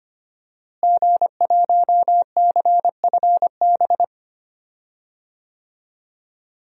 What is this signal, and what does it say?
Morse code, heard as a keyed tone, about 25 wpm, 700 Hz. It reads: G1CFB
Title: DXCC-100 Call Signs - 5 Characters - Repeated 2x - 25wpm